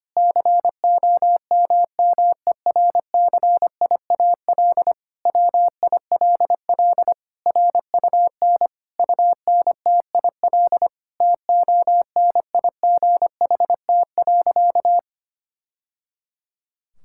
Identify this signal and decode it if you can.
Morse code, heard as a keyed tone, about 25 wpm, 700 Hz. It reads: COMMERCIAL WILL RUN UNTIL TONIGHT.
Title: Sentences from Top 1000 Words - Review Part 5 - 25wpm